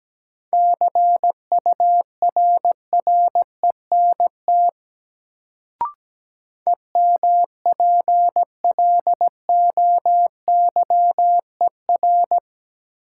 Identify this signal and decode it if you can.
Morse code, heard as a keyed tone, about 17 wpm, 700 Hz. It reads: CURRENT EMPLOYER
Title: Sentences from Top 1600 Words - New Word Drill Alt - 17wpm